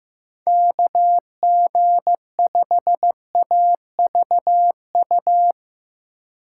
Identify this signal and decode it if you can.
Morse code, heard as a keyed tone, about 17 wpm, 700 Hz. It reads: KG5AVU
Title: Speed Racing - USA 2x3 Call Signs 15wpm